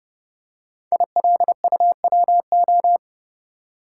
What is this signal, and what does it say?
Morse code, heard as a keyed tone, about 30 wpm, 700 Hz. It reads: ILUWO
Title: Code Groups - 5 Letters - 30wpm